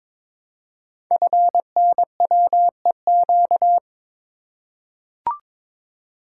Morse code, heard as a keyed tone, about 21 wpm, 700 Hz. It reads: FNWEQ
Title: Code Groups - 5 Letters - 22wpm